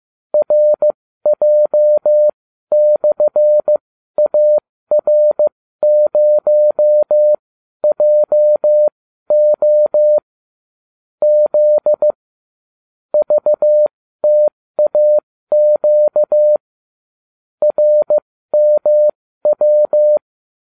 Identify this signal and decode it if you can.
Morse code, heard as a keyed tone, about 15 wpm, spaced as for 12 wpm, 600 Hz. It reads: RJ/AR0JO Z VTAQ RMW